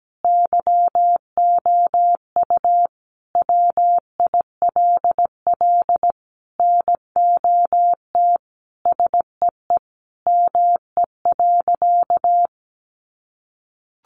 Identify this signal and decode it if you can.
Morse code, heard as a keyed tone, about 17 wpm, 700 Hz. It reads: YOU WILL NOT SEE ME.
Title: Sentences from Top 600 Words 17wpm